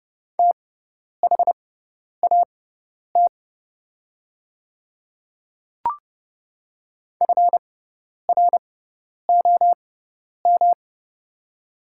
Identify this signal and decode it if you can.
Morse code, heard as a keyed tone, about 30 wpm, spaced as for 10 wpm, 700 Hz. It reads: THAT FROM
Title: 6x Character Spacing - Top 100 Words - 30wpm